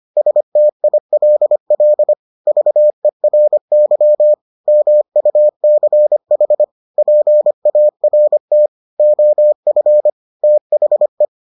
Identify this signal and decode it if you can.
Morse code, heard as a keyed tone, about 25 wpm, 600 Hz. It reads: STILL VERY MUCH PART OF THE